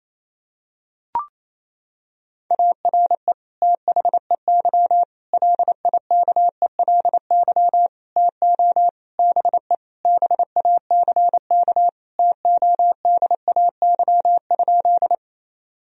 Morse code, heard as a keyed tone, about 28 wpm, 700 Hz. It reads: ARE THEY LIKELY TO BE BACK TODAY?
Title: Sentences from Top 1000 Words - Review Part 4 - 28wpm